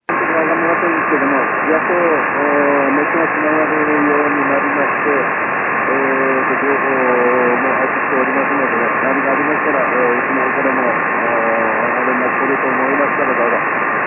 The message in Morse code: none

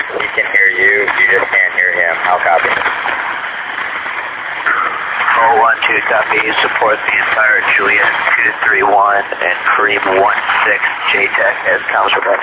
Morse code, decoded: none

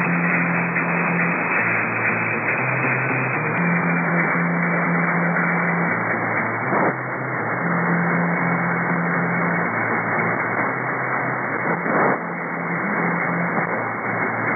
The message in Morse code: none

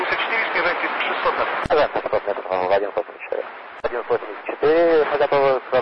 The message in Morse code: none

